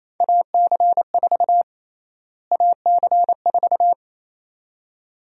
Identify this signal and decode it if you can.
Morse code, heard as a keyed tone, about 28 wpm, 700 Hz. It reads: AC4 AC4
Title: USA Call Sign Prefixes - Repeated 2x - 28wpm